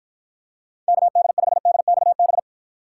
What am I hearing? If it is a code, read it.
Morse code, heard as a keyed tone, about 53 wpm, 700 Hz. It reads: KD5DXB